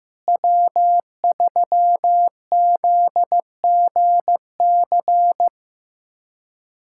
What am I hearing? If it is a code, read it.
Morse code, heard as a keyed tone, about 16 wpm, 700 Hz. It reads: W3ZGC
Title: USA Callsign DE Callsign 15wpm